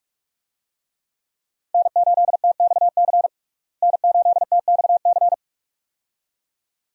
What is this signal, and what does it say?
Morse code, heard as a keyed tone, about 46 wpm, 700 Hz. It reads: N8TXC N8TXC
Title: USA Call Signs - 5 Characters - Repeated 2x - 45wpm